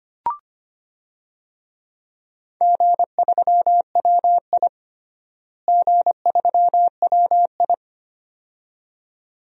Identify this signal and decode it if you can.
Morse code, heard as a keyed tone, about 25 wpm, 700 Hz. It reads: G3WI G3WI
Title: DXCC-100 Call Signs - 4 Characters - Repeated 2x - 25wpm